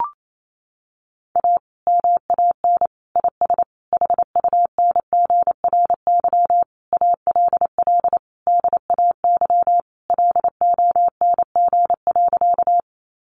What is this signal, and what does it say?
Morse code, heard as a keyed tone, about 28 wpm, 700 Hz. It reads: A MAN IS HUNGRY ALL DAY LONG.
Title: Sentences from Top 1400 Words 28wpm